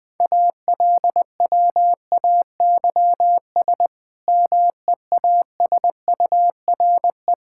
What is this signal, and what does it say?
Morse code, heard as a keyed tone, about 20 wpm, 700 Hz. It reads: ALWAYS MEASURE
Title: Difficult Sentences with Hints - 20wpm